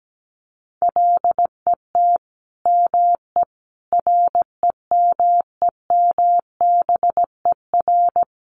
Morse code, heard as a keyed tone, about 17 wpm, 700 Hz. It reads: LET ME REMEMBER
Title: Sentences from Top 400 Words 17wpm